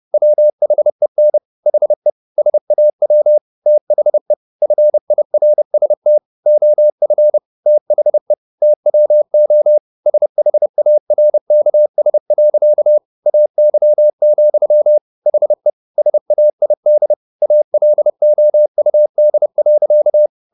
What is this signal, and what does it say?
Morse code, heard as a keyed tone, about 30 wpm, 600 Hz. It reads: WHEN HE SAW THE FIRST OF THE TWO SHARKS. AY, HE SAID ALOUD.